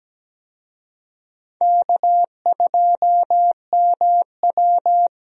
Morse code, heard as a keyed tone, about 17 wpm, 700 Hz. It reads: K2MW
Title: USA Call Signs - 4 Characters - Repeated 2x - 17wpm